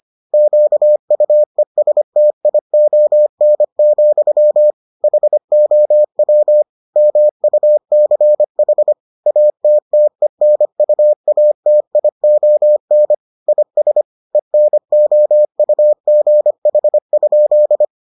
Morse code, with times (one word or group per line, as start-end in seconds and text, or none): 0.3-4.7 QUESTION,
5.0-6.6 HOW
7.0-8.9 MUCH
9.3-13.1 ATTENUATION
13.5-14.0 IS
14.3-17.8 ENOUGH?